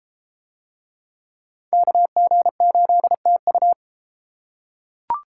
Morse code, heard as a keyed tone, about 30 wpm, 700 Hz. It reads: KG8TU